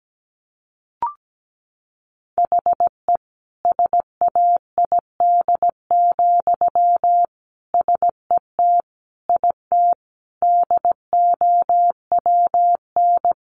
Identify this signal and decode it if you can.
Morse code, heard as a keyed tone, about 17 wpm, 700 Hz. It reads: HE SAID, SET IT DOWN
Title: Sentences from Top 1400 Words 17wpm